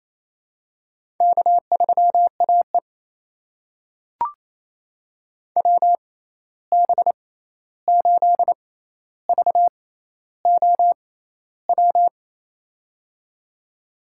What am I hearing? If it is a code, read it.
Morse code, heard as a keyed tone, about 28 wpm, spaced as for 13 wpm, 700 Hz. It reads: K3AE WB8VOW
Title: USA Call Signs - 6x Character Spacing - 28wpm